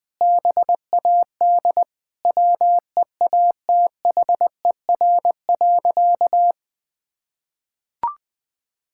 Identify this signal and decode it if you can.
Morse code, heard as a keyed tone, about 19 wpm, 700 Hz. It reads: BAD WEATHER.
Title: Sentences from Top 1000 Words - Review Part 6 - 20wpm